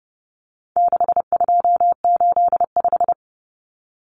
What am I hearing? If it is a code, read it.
Morse code, heard as a keyed tone, about 30 wpm, 700 Hz. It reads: 6285
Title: Common Cut Numbers - 4-Digit 30wpm